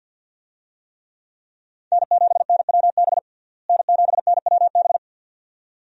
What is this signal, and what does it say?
Morse code, heard as a keyed tone, about 47 wpm, 700 Hz. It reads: N7NWB N7NWB
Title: USA Call Signs - 5 Characters - Repeated 2x - 50wpm